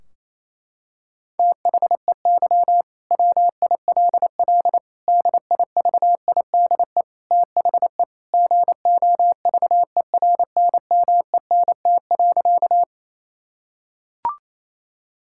Morse code, heard as a keyed tone, about 28 wpm, 700 Hz. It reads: THEY WILL DIVIDE THE GOVERNMENT.
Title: Sentences from Top 1600 Words 28wpm